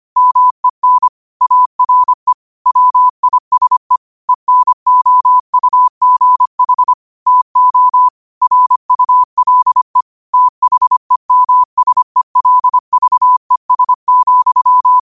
Morse code, heard as a keyed tone, about 25 wpm, 1000 Hz. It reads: MEN ARE WISE ENOUGH TO RULE THEMSELVES,
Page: New quotes every day in morse code at 25 Words per minute.